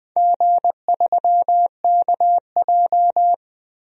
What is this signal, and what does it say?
Morse code, heard as a keyed tone, about 21 wpm, 700 Hz. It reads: G3KJ
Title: DXCC-100 Call Signs - 4 Characters - Repeated 2x - 20wpm